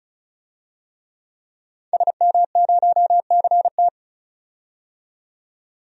Morse code, heard as a keyed tone, about 34 wpm, 700 Hz. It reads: SM0CT
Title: DXCC-100 Call Signs - 5 Characters - 35wpm